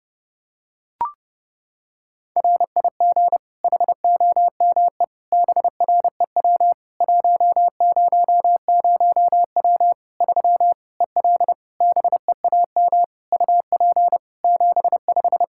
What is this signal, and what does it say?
Morse code, heard as a keyed tone, about 30 wpm, 700 Hz. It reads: RIG HOME BREW 100W 3 EL BEAM UP 75
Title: K1USN SST QSO Practice 30wpm